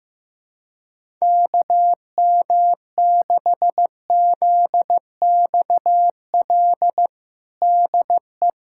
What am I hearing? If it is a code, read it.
Morse code, heard as a keyed tone, about 15 wpm, 700 Hz. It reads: KM6ZXL DE
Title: USA Callsign DE Callsign 15wpm